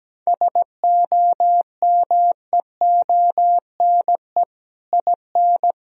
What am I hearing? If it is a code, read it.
Morse code, heard as a keyed tone, about 17 wpm, 700 Hz. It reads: SOMEONE IN
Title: Sentences from Top 1000 Words - Review Part 1 - 17wpm